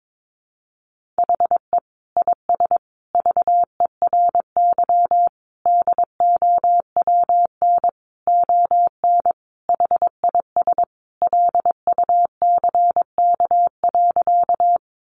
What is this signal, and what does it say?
Morse code, heard as a keyed tone, about 22 wpm, 700 Hz. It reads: HE IS VERY DOWN ON HIS LUCK.
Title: Sentences from Top 1500 Words 22wpm